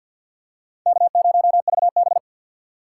Morse code, heard as a keyed tone, about 53 wpm, 700 Hz. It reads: K0VB
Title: USA Call Signs - 4 Characters - 3x Character Spacing - 50wpm